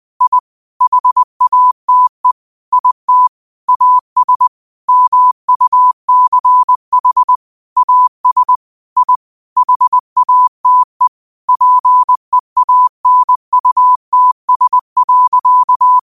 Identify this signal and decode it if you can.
Morse code, heard as a keyed tone, about 20 wpm, 1000 Hz. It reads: I HATE IT AS MUCH AS I HATE PEANUTS.
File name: qotdfiles / 20wpm